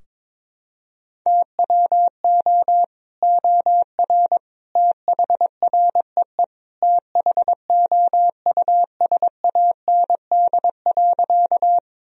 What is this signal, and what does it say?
Morse code, heard as a keyed tone, about 22 wpm, 700 Hz. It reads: TWO OR THREE THOUSAND.
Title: Sentences from Top 1000 Words - Review Part 3 - 22wpm